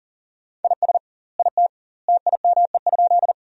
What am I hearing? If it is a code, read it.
Morse code, heard as a keyed tone, about 40 wpm, 700 Hz. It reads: IS IT TIME?
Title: 2x Word-Spacing - Sentences from Top 100 Words - 40wpm